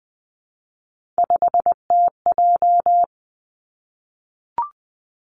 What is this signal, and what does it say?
Morse code, heard as a keyed tone, about 20 wpm, 700 Hz. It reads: 5TJ
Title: Three Challenging Letter-Number Combinations 20wpm